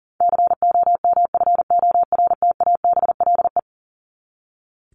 Morse code, heard as a keyed tone, about 40 wpm, 700 Hz. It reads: COMFORTABLE
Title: Top 751-1000 Words 40wpm